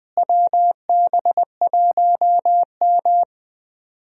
Morse code, heard as a keyed tone, about 20 wpm, 700 Hz. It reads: WB1M
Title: USA Call Signs - 4 Characters - Repeated 2x - 20wpm